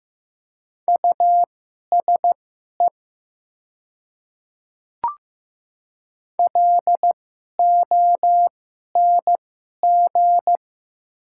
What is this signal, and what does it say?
Morse code, heard as a keyed tone, about 15 wpm, spaced as for 11 wpm, 700 Hz. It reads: USE LONG
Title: 2x Character Spacing - Top 100 Words - 15wpm